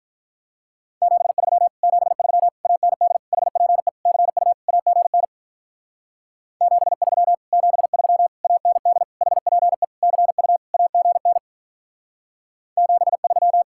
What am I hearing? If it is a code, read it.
Morse code, heard as a keyed tone, about 44 wpm, 700 Hz. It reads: 73 73 AND HPE CU AGN 73 73 AND HPE CU AGN 73